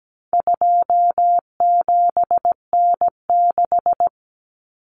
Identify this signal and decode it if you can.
Morse code, heard as a keyed tone, about 17 wpm, 700 Hz. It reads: 27N6
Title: Common Cut Numbers - 4-Digit 17wpm